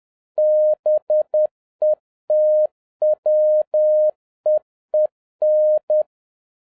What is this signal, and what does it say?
Morse code, heard as a keyed tone, about 10 wpm, 610 Hz. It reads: BETWEEN
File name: 2011-07-mp3 / n31 THE 3.11 2 10wpm